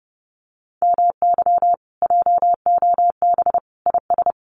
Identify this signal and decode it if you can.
Morse code, heard as a keyed tone, about 30 wpm, 700 Hz. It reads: MY JOB IS
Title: Common QSO Phrases Encore 30wpm